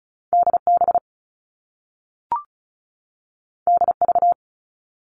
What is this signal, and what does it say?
Morse code, heard as a keyed tone, about 36 wpm, 690 Hz. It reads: DB DV
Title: All Cut Numbers - 2-Digit 35wpm